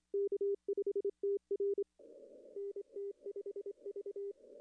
Morse code, heard as a keyed tone, about 26 wpm, 410 Hz. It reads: K5TR NT5V
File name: contests / arrl-160-k5tr-op-w5tn -0400